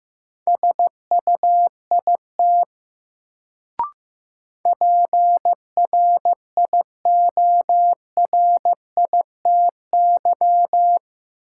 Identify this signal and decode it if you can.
Morse code, heard as a keyed tone, about 15 wpm, 700 Hz. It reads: SUIT PRIORITY
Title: Sentences from Top 1600 Words - New Word Drill Alt - 15wpm